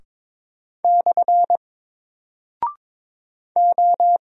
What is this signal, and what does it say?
Morse code, heard as a keyed tone, about 22 wpm, 700 Hz. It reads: / O
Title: Single Character (Letter:Number:Punctuation:Prosign) 22wpm